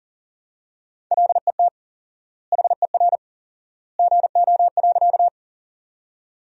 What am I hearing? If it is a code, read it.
Morse code, heard as a keyed tone, about 40 wpm, 700 Hz. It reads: LET HER GO.
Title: 4x Word-Spacing - Sentences from Top 300 Words - 40wpm